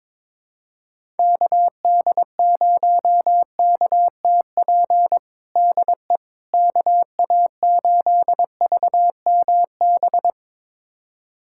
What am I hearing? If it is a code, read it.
Morse code, heard as a keyed tone, about 22 wpm, 700 Hz. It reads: KD0KTP DE KA8VMB
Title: USA Callsign DE Callsign 22wpm